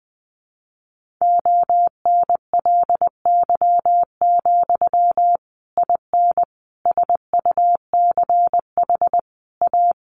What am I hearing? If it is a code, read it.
Morse code, heard as a keyed tone, about 19 wpm, 700 Hz. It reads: ONLY, IN SUCH A
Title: Sentences from Top 1500 Words 20wpm